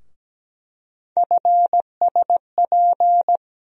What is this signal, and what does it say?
Morse code, heard as a keyed tone, about 17 wpm, 700 Hz. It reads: FSP
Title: USA Call Sign Postfixes - 3 Letters 17wpm